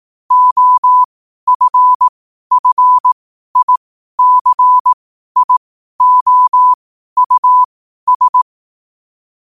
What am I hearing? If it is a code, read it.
Morse code, heard as a keyed tone, about 18 wpm, spaced as for 12 wpm, 1000 Hz. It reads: OFFICIOUS
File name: 12wpm